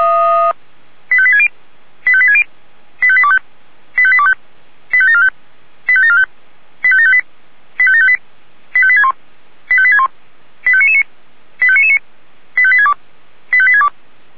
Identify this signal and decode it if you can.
Five-tone selective call (ZVEI), heard as quick runs of five tones, tones of about 70 ms. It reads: 85860 85860 85825 85825 85745 85745 85758 85758 85861 85861 85999 85999 85862 85862
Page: SDR/ZVEI